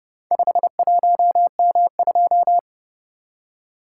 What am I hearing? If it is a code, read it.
Morse code, heard as a keyed tone, about 30 wpm, 700 Hz. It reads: 51M2